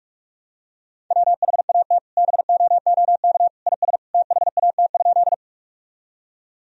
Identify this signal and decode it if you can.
Morse code, heard as a keyed tone, about 46 wpm, 700 Hz. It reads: WHAT BOOK IS THAT?
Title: Sentences from Top 1000 Words - Review Part 1 - 45wpm